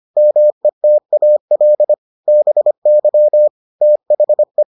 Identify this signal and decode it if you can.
Morse code, heard as a keyed tone, about 25 wpm, 600 Hz. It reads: METAL BY THE